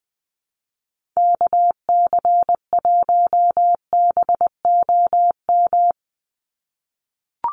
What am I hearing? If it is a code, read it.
Morse code, heard as a keyed tone, about 20 wpm, 700 Hz. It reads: KC1BOM
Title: USA Call Signs 20wpm